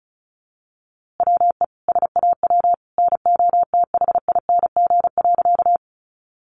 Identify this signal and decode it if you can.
Morse code, heard as a keyed tone, about 34 wpm, 700 Hz. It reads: WE SAW NOTHING.
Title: A Sound Of Thunder 35wpm